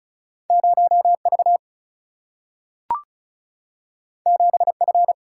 Morse code, heard as a keyed tone, about 36 wpm, 700 Hz. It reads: 0 V 7 F
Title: Single-Letter Single-Number 35wpm